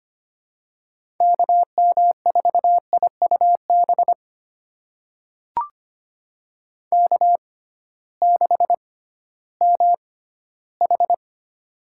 Morse code, heard as a keyed tone, about 25 wpm, spaced as for 14 wpm, 700 Hz. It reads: KM4IUB K6MH
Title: USA Call Signs - 6x Character Spacing - 25wpm